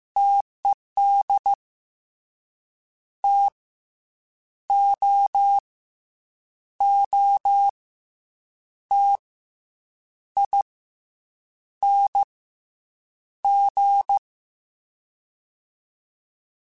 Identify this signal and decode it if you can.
Morse code, heard as a keyed tone, about 15 wpm, spaced as for 7 wpm, 800 Hz. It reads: TED TOOTING
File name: Difficult Combos / Difficult Combos-mp3s / T AOEI @13